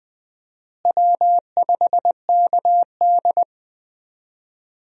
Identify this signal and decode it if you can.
Morse code, heard as a keyed tone, about 20 wpm, 700 Hz. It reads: W5KD
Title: USA Call Signs - 4 Characters - Repeated 3x - 20wpm